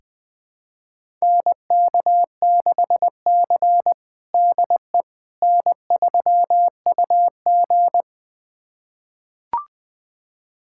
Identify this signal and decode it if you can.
Morse code, heard as a keyed tone, about 20 wpm, 700 Hz. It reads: NK6C DE N3UG
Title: USA Callsign DE Callsign - 4 Characters - 20wpm